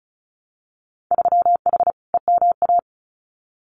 Morse code, heard as a keyed tone, about 36 wpm, 700 Hz. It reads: 3H EMA
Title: Winter Field Day Exchanges Encore 35wpm